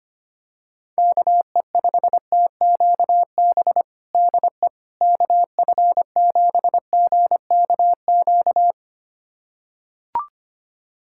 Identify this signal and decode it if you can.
Morse code, heard as a keyed tone, about 25 wpm, 700 Hz. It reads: KE5TQB DE KF7GKQ
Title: USA Callsign DE Callsign - 6 Characters - 25wpm